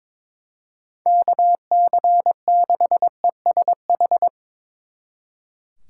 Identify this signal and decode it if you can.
Morse code, heard as a keyed tone, about 25 wpm, 700 Hz. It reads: KC6ESH
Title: Speed Racing - USA Call Signs 22wpm